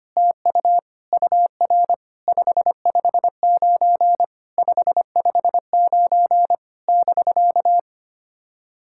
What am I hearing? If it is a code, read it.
Morse code, heard as a keyed tone, about 25 wpm, 660 Hz. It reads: TU UR 559 559 <BK>
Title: SOTA QSO Practice 25wpm